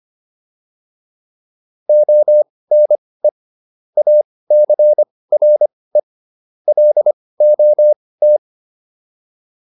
Morse code, heard as a keyed tone, about 25 wpm, spaced as for 18 wpm, 600 Hz. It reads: ONE ACRE LOT